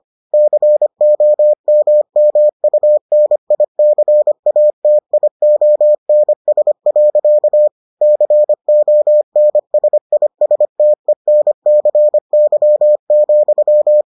Morse code, heard as a keyed tone, about 25 wpm, 600 Hz. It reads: COMMUNICATIONS. CONSISTENCY,